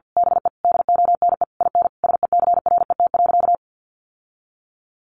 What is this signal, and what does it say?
Morse code, heard as a keyed tone, about 50 wpm, 700 Hz. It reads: BE DONE IN SECRET.
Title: Sentences from Top 1200 Words 50wpm